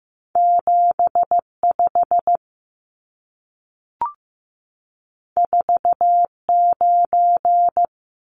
Morse code, heard as a keyed tone, about 15 wpm, 700 Hz. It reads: 75 49
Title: Numbers - 2-Digit 15wpm